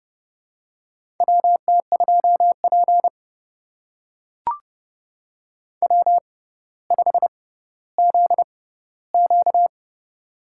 Morse code, heard as a keyed tone, about 32 wpm, spaced as for 16 wpm, 700 Hz. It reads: WT2P W5ZQ
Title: USA Call Signs - 4 Characters - 6x Character Spacing - 30wpm